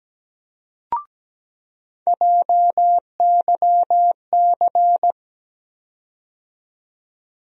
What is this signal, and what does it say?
Morse code, heard as a keyed tone, about 17 wpm, 700 Hz. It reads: JYC
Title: USA Call Sign Postfixes - 3 Letters 17wpm